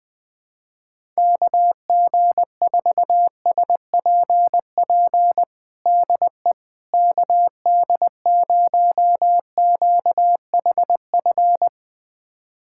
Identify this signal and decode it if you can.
Morse code, heard as a keyed tone, about 20 wpm, 700 Hz. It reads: KG4SPP DE KD0QHF
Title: USA Callsign DE Callsign - 6 Characters - 20wpm